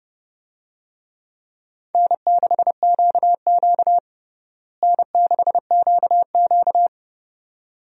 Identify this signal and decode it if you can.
Morse code, heard as a keyed tone, about 32 wpm, 700 Hz. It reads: N6QQ N6QQ
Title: USA Call Signs - 4 Characters - Repeated 2x - 30wpm